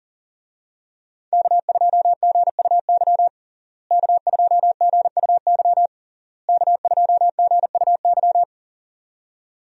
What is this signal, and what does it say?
Morse code, heard as a keyed tone, about 40 wpm, 700 Hz. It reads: K2GUY K2GUY K2GUY
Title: USA Call Signs - 5 Characters - Repeated 3x - 40wpm